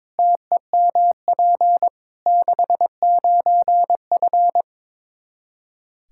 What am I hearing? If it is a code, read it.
Morse code, heard as a keyed tone, about 22 wpm, 700 Hz. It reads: TEMP 69F
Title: K1USN SST QSO Practice 22wpm